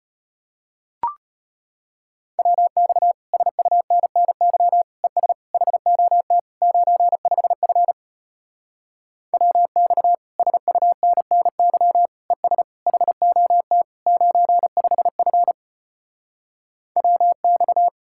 Morse code, heard as a keyed tone, about 33 wpm, 700 Hz. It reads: WX SUNNY ES HOT 95F WX SUNNY ES HOT 95F WX